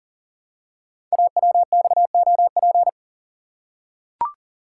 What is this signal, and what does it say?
Morse code, heard as a keyed tone, about 42 wpm, 700 Hz. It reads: AWXOP
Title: Code Groups - 5 Letters - 40wpm